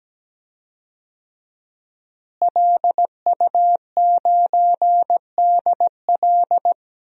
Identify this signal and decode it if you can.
Morse code, heard as a keyed tone, about 17 wpm, 700 Hz. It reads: LU9DL
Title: DXCC-100 Call Signs - 5 Characters - Repeated 2x - 17wpm